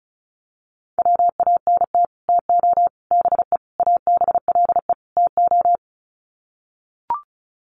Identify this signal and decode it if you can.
Morse code, heard as a keyed tone, about 34 wpm, 700 Hz. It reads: WANT TO BE ABLE TO
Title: Sets of 5 Words 35wpm